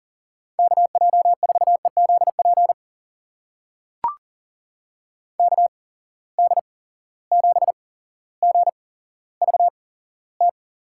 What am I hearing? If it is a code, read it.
Morse code, heard as a keyed tone, about 40 wpm, spaced as for 17 wpm, 700 Hz. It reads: KJ4EZP KD7GVT
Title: USA Call Signs - 6 Characters - 8x Character Spacing - 40wpm